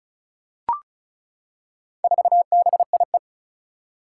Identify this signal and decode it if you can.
Morse code, heard as a keyed tone, about 36 wpm, 700 Hz. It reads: 4BIE
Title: Four Challenging Letter-Number Combinations 35wpm